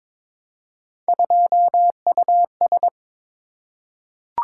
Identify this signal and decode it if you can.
Morse code, heard as a keyed tone, about 22 wpm, 700 Hz. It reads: 2US
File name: practice / 22wpm / Three Letter-Number Combinations - 22wpm